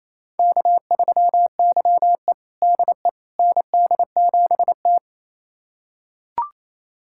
Morse code, heard as a keyed tone, about 27 wpm, 700 Hz. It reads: K3YE DE ND7T
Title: USA Callsign DE Callsign - 4 Characters - 28wpm